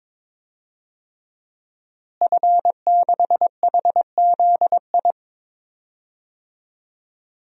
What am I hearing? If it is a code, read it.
Morse code, heard as a keyed tone, about 22 wpm, 700 Hz. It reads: F6HZI
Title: DXCC-100 Call Signs - 5 Characters - Repeated 3x - 22wpm